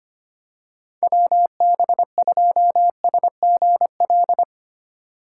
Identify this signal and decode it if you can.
Morse code, heard as a keyed tone, about 25 wpm, 700 Hz. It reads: WB2SGL
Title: USA Call Signs - 3x Character Spacing - 25wpm